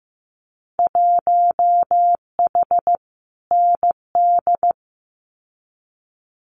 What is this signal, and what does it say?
Morse code, heard as a keyed tone, about 15 wpm, 700 Hz. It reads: 1H ND
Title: Winter Field Day Exchanges 15wpm